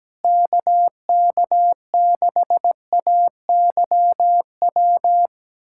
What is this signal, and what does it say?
Morse code, heard as a keyed tone, about 17 wpm, 700 Hz. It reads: KK6AYW
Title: USA Callsign DE Callsign - 6 Characters - 17wpm